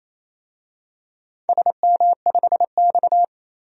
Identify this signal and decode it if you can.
Morse code, heard as a keyed tone, about 28 wpm, 700 Hz. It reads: SM5X
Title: DXCC-100 Call Signs - 4 Characters - 28wpm